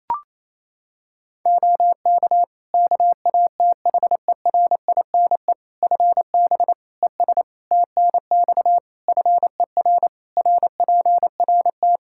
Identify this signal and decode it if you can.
Morse code, heard as a keyed tone, about 28 wpm, 700 Hz. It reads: OK KATHERINE FB ES TNX FER RPRT
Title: K1USN SST QSO Practice 28wpm